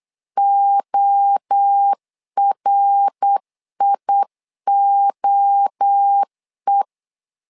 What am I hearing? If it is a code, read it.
Morse code, heard as a keyed tone, about 8 wpm, 800 Hz. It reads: ORIOE
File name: LEZIONI / 4 Q1 8wpm 3035